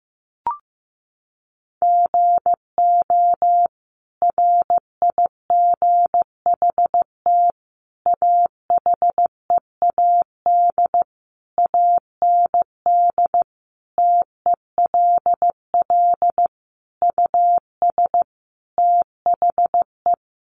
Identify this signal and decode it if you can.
Morse code, heard as a keyed tone, about 15 wpm, 700 Hz. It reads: GO RIGHT AHEAD AND TELL US THE
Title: Sentences from Top 1100 Words 15wpm